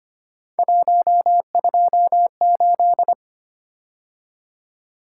Morse code, heard as a keyed tone, about 25 wpm, 700 Hz. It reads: 128
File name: Three Letter-Number Combinations - 25wpm